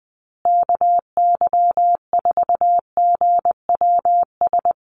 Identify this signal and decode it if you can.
Morse code, heard as a keyed tone, about 20 wpm, 700 Hz. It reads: KY4GWS
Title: USA Call Signs Repeated 3x 20wpm